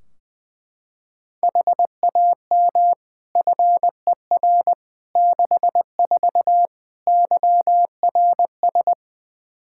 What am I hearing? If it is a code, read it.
Morse code, heard as a keyed tone, about 20 wpm, 700 Hz. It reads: HAM FER 64 YRS
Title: K1USN SST QSO Practice 20wpm